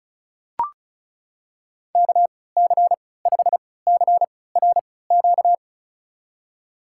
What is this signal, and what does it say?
Morse code, heard as a keyed tone, about 36 wpm, spaced as for 20 wpm, 700 Hz. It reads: KC5CRQ
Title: USA Call Signs - 6 Characters - 3x Character Spacing - 35wpm